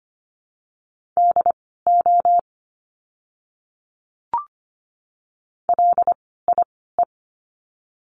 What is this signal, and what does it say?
Morse code, heard as a keyed tone, about 25 wpm, spaced as for 16 wpm, 700 Hz. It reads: DO LIE
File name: course / 25:10wpm / 094 Lesson 16f - Letter c - Words 25wpm